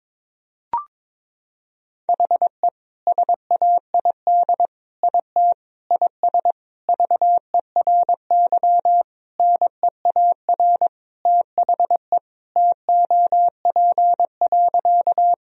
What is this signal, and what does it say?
Morse code, heard as a keyed tone, about 22 wpm, 700 Hz. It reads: HE SAID IT IS VERY NEAR THE TOP.
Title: Sentences from Top 1000 Words - Review Part 2 - 22wpm